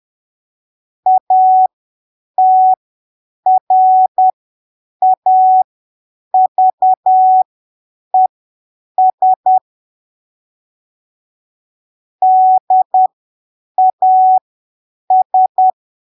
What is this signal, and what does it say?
Morse code, heard as a keyed tone, about 10 wpm, spaced as for 7 wpm, 750 Hz. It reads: ATRAVES DAS
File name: arquivos / texto / texto 23 10ppm